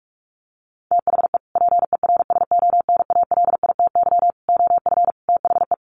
Morse code, heard as a keyed tone, about 44 wpm, 700 Hz. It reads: THE PERSONALITY OF THE
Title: Sentences from Top 1400 Words 45wpm